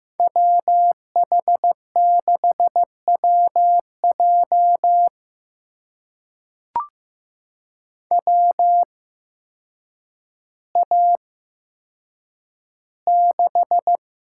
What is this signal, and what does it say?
Morse code, heard as a keyed tone, about 15 wpm, spaced as for 8 wpm, 700 Hz. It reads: WH6WJ WA6
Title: USA Call Signs - 5 Characters - 8x Character Spacing - 15wpm